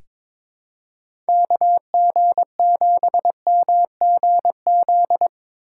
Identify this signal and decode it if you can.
Morse code, heard as a keyed tone, about 22 wpm, 700 Hz. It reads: KG7MGZ
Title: USA Call Signs - 6 Characters - 3x Character Spacing - 22wpm